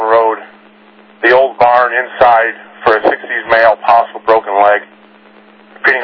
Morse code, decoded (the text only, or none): none